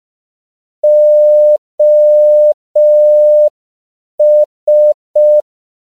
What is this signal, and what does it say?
Morse code, heard as a keyed tone, about 5 wpm, 600 Hz. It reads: OS